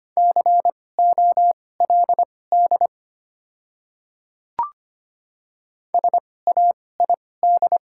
Morse code, heard as a keyed tone, about 25 wpm, spaced as for 18 wpm, 690 Hz. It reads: COLD SAID
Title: 2x Character Spacing - Top 500 Words - 25wpm